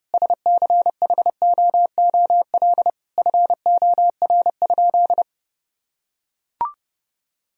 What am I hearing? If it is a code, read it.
Morse code, heard as a keyed tone, about 32 wpm, 700 Hz. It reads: SCHOOL FOR?
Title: Sentences from Top 1000 Words - Review Part 1 - 30wpm